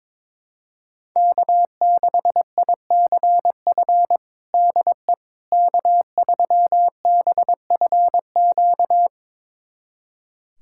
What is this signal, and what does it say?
Morse code, heard as a keyed tone, about 22 wpm, 700 Hz. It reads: K6ICF DE K3BFQ
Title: USA Callsign DE Callsign - 5 Characters - 22wpm